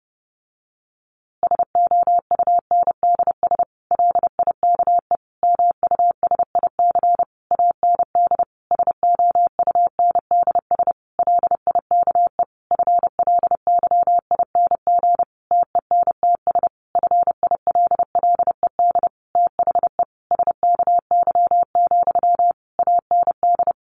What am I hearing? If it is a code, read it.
Morse code, heard as a keyed tone, about 30 wpm, 700 Hz. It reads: SOUNDS LIKE MUSIC AND SOUNDS LIKE FLYING TENTS FILLED THE SKY, AND
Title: A Sound Of Thunder 30wpm